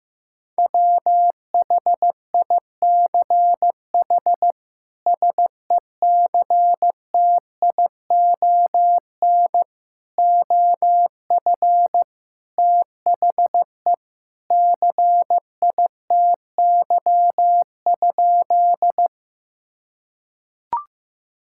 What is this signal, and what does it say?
Morse code, heard as a keyed tone, about 15 wpm, 700 Hz. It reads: WHICH SECTION OF THE CITY?
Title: Sentences from Top 1000 Words - Review Part 5 - 15wpm